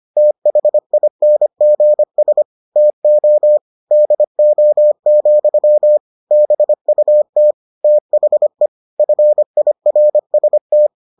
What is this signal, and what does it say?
Morse code, heard as a keyed tone, about 25 wpm, 600 Hz. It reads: THINGS TO DO, BUT THE FIRST